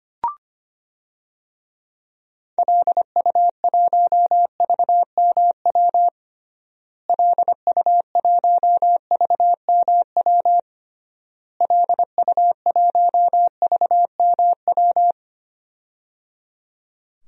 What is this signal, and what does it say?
Morse code, heard as a keyed tone, about 25 wpm, 700 Hz. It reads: LU1VMW LU1VMW LU1VMW
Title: DXCC-100 Call Signs - 6 Characters - Repeated 3x - 25wpm